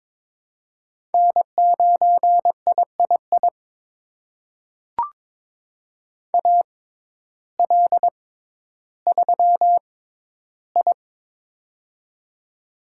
Morse code, heard as a keyed tone, about 22 wpm, spaced as for 12 wpm, 700 Hz. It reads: N9III AL3I
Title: USA Call Signs - 6x Character Spacing - 22wpm